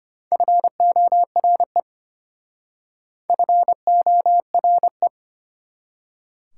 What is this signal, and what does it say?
Morse code, heard as a keyed tone, about 33 wpm, 700 Hz. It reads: FORE FORE